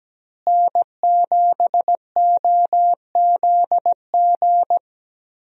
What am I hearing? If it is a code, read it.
Morse code, heard as a keyed tone, about 17 wpm, 700 Hz. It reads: N7OZG
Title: USA Call Signs - 5 Characters - Repeated 3x - 17wpm